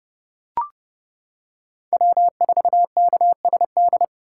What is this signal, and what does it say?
Morse code, heard as a keyed tone, about 29 wpm, 700 Hz. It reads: W4KSD